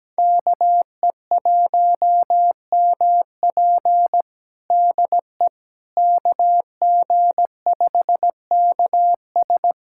Morse code, heard as a keyed tone, about 17 wpm, 700 Hz. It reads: KE1MP DE KG5KS
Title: USA Callsign DE Callsign - 5 Characters - 17wpm